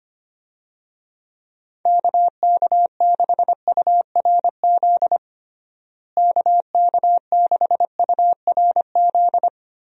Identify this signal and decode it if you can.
Morse code, heard as a keyed tone, about 25 wpm, 700 Hz. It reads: KK6URZ KK6URZ
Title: USA Call Signs - 6 Characters - Repeated 3x - 25wpm